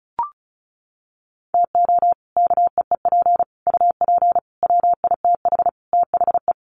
Morse code, heard as a keyed tone, about 34 wpm, 700 Hz. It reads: TO KEEP UP WITH THE
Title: Sets of 5 Words Encore 35wpm